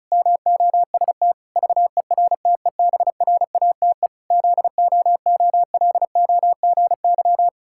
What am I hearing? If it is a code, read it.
Morse code, heard as a keyed tone, about 36 wpm, 700 Hz. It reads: MOST VERTEBRATE ZOOLOGY
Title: Pangram Challenge v2 - 35wpm